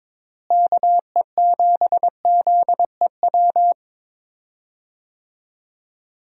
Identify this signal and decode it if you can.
Morse code, heard as a keyed tone, about 22 wpm, 700 Hz. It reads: KE7ZEW
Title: USA Call Signs - 6 Characters - Repeated 3x - 22wpm